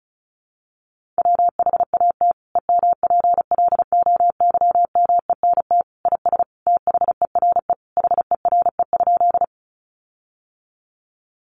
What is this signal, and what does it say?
Morse code, heard as a keyed tone, about 34 wpm, 700 Hz. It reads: WHAT EMPLOYMENT IS THERE HERE?
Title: Sentences from Top 1500 Words 35wpm